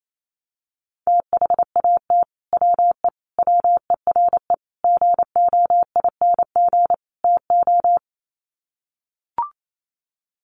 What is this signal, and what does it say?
Morse code, heard as a keyed tone, about 28 wpm, 700 Hz. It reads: THAT WE WERE GOING TO
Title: Sets of 5 Words Encore2 28wpm